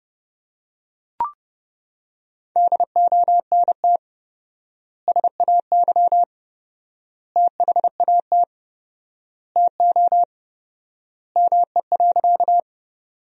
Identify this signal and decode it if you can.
Morse code, heard as a keyed tone, about 31 wpm, 700 Hz. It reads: DONT SAY THAT TO ME.
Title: 4x Word-Spacing - Sentences from Top 300 Words - 30wpm